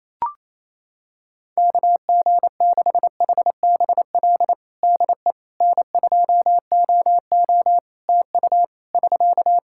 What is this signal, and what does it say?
Morse code, heard as a keyed tone, about 28 wpm, 700 Hz. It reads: KG6HBL DE N2OO TU <SK>
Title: K1USN SST QSO Practice 28wpm